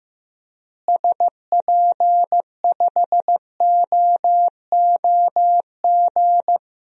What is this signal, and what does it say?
Morse code, heard as a keyed tone, about 15 wpm, 700 Hz. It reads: SP5OOG
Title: DXCC-100 Call Signs - 6 Characters - Repeated 3x - 15wpm